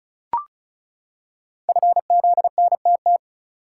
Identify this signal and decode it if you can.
Morse code, heard as a keyed tone, about 36 wpm, 700 Hz. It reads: FZNTT
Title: Code Groups - 5 Letters - 35wpm